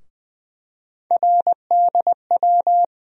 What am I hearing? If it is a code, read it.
Morse code, heard as a keyed tone, about 20 wpm, 700 Hz. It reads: RDW
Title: USA Call Sign Postfixes - 3 Letters 20wpm